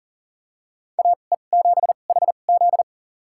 Morse code, heard as a keyed tone, about 40 wpm, spaced as for 29 wpm, 700 Hz. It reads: AE7HZ
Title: USA Call Signs - 5 Characters - 2x Character Spacing - 40wpm